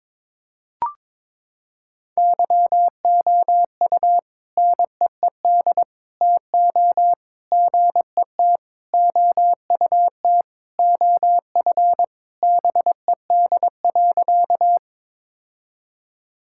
Morse code, heard as a keyed tone, about 22 wpm, 700 Hz. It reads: YOU NEED TO GET OUT OF BED.
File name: practice / 22wpm / Sentences with 3-Word Phrases - 22wpm